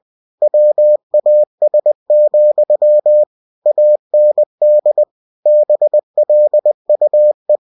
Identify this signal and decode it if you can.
Morse code, heard as a keyed tone, about 20 wpm, 600 Hz. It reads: WAS, AND BLUE
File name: cw-books / winnie-the-pooh / winnie-20wpm-page-0109